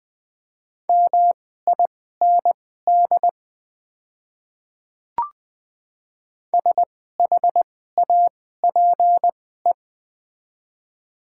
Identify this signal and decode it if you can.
Morse code, heard as a keyed tone, about 20 wpm, spaced as for 14 wpm, 700 Hz. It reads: MIND SHAPE
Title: 2x Character Spacing - Top 500 Words - 20wpm